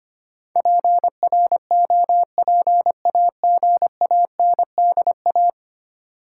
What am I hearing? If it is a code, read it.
Morse code, heard as a keyed tone, about 25 wpm, 700 Hz. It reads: PROPAGANDA
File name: Ten Letter Words - 25wpm